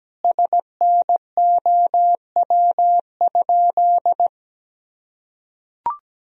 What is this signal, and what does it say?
Morse code, heard as a keyed tone, about 17 wpm, 700 Hz. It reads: SNOW?
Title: Sentences from Top 1000 Words - Review Part 3 - 17wpm